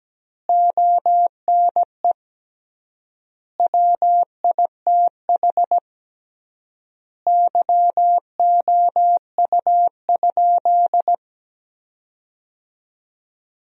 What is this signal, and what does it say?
Morse code, heard as a keyed tone, about 17 wpm, 700 Hz. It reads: ONE WITH YOU?
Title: 3x Word-Spacing - Sentences from Top 500 Words - 17wpm